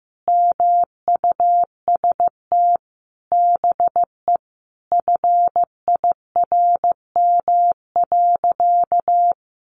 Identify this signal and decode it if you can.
Morse code, heard as a keyed tone, about 15 wpm, 700 Hz. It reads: MUST BE FIRM.
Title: Sentences from Top 1400 Words 15wpm